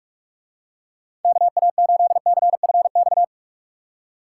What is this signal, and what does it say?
Morse code, heard as a keyed tone, about 44 wpm, 700 Hz. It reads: KA8CFX